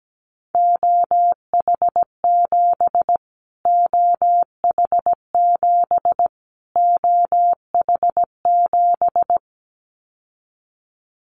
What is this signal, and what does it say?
Morse code, heard as a keyed tone, about 17 wpm, 700 Hz. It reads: OH7 OH7 OH7
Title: Callsign Prefixes for Top 25 DXCC - Repeated 3x - 17wpm